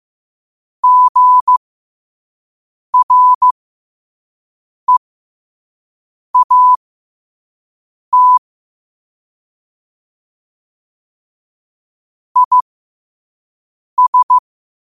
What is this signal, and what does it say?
Morse code, heard as a keyed tone, about 15 wpm, spaced as for 5 wpm, 1000 Hz. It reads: GREAT IS
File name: qotdfiles / 5wpm